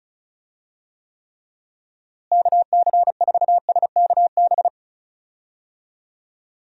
Morse code, heard as a keyed tone, about 34 wpm, 700 Hz. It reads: KC4SKB
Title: USA Call Signs - 6 Characters - Repeated 2x - 35wpm